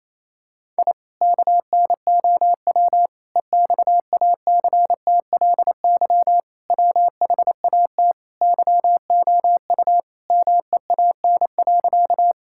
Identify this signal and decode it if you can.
Morse code, heard as a keyed tone, about 28 wpm, 700 Hz. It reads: I KNOW EXACTLY WHAT YOU MEAN.
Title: Sentences from Top 1000 Words - Review Part 6 - 28wpm